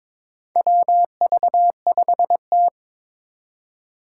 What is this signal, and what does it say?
Morse code, heard as a keyed tone, about 22 wpm, 700 Hz. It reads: WV5T
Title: Four Challenging Letter-Number Combinations 22wpm